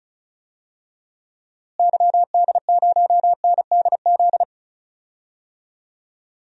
Morse code, heard as a keyed tone, about 35 wpm, 700 Hz. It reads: YD0NDZ
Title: DXCC-100 Call Signs - 6 Characters - Repeated 3x - 35wpm